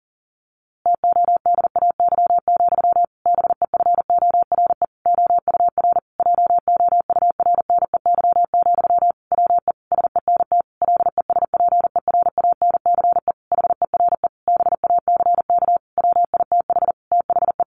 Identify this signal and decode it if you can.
Morse code, heard as a keyed tone, about 40 wpm, 700 Hz. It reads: TODAY, BEFORE OUR JOURNEY, WE SENT LESPERANCE HERE BACK WITH THE
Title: A Sound Of Thunder 40wpm